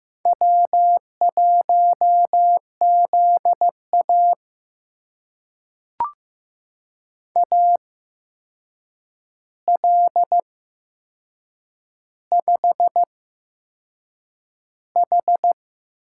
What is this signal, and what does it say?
Morse code, heard as a keyed tone, about 15 wpm, spaced as for 6 wpm, 700 Hz. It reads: W1ZA AL5H
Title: USA Call Signs - 4 Characters - 8x Character Spacing - 15wpm